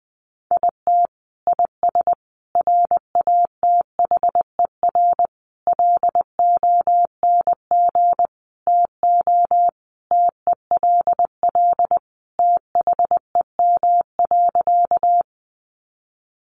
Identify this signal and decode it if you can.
Morse code, heard as a keyed tone, about 19 wpm, 700 Hz. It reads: IT IS RATHER LONG TO TELL THEM.
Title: Sentences from Top 700 Words 20wpm